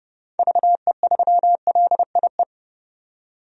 Morse code, heard as a keyed tone, about 30 wpm, 700 Hz. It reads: VE3LIE
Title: DXCC-100 Call Signs - 6 Characters - 30wpm